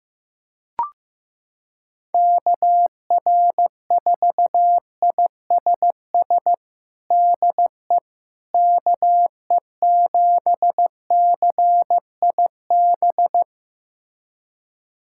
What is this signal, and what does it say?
Morse code, heard as a keyed tone, about 15 wpm, 700 Hz. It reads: KR4ISS DE KE7CIB
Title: USA Callsign DE Callsign - 6 Characters - 15wpm